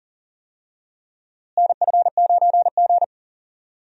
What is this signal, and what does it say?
Morse code, heard as a keyed tone, about 40 wpm, 700 Hz. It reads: NF9G
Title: USA Call Signs - 4 Characters - Repeated 3x - 40wpm